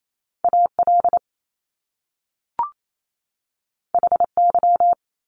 Morse code, heard as a keyed tone, about 28 wpm, 700 Hz. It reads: A L H Y
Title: Top Two Letter Combinations in Words 28wpm